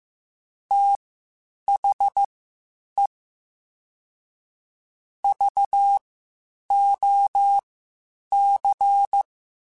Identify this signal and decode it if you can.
Morse code, heard as a keyed tone, about 15 wpm, spaced as for 8 wpm, 800 Hz. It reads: THE VOC
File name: (spaced) / QSOs Spaced-mp3s / WD0BGP 1 @13 sp